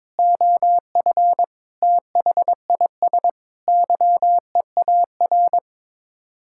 Morse code, heard as a keyed tone, about 22 wpm, 700 Hz. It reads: OF THIS YEAR
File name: practice / 22wpm / Sets of 8 Words - 22wpm